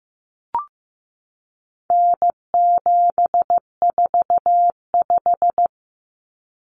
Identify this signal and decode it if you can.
Morse code, heard as a keyed tone, about 15 wpm, 700 Hz. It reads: N745
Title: Common Cut Numbers - 4-Digit 15wpm